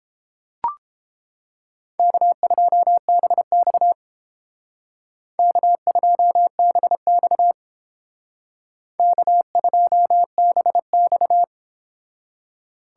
Morse code, heard as a keyed tone, about 30 wpm, 700 Hz. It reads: K2BX K2BX K2BX